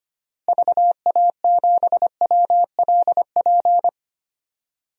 Morse code, heard as a keyed tone, about 25 wpm, 700 Hz. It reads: VA7WLP
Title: DXCC-100 Call Signs - 6 Characters - Repeated 2x - 25wpm